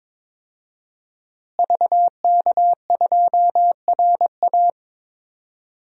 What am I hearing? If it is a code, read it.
Morse code, heard as a keyed tone, about 22 wpm, 700 Hz. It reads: VK2RA
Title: DXCC-100 Call Signs - 5 Characters - Repeated 3x - 22wpm